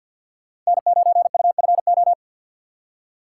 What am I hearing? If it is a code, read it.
Morse code, heard as a keyed tone, about 63 wpm, 700 Hz. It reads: N9UFO